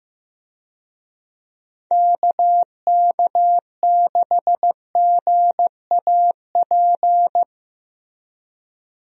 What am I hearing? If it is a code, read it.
Morse code, heard as a keyed tone, about 15 wpm, 700 Hz. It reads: KK6GAP
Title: USA Call Signs - 6 Characters - Repeated 3x - 15wpm